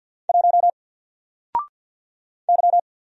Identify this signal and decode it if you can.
Morse code, heard as a keyed tone, about 50 wpm, 700 Hz. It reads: 1 Y
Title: Single Letter-Number 50wpm